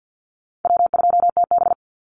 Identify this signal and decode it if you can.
Morse code, heard as a keyed tone, about 50 wpm, 700 Hz. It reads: W2TB
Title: Easy Call Signs - 50wpm